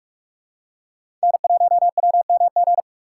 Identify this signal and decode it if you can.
Morse code, heard as a keyed tone, about 48 wpm, 700 Hz. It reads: N1WMG
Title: USA Call Signs - 5 Characters - 4x Character Spacing - 45wpm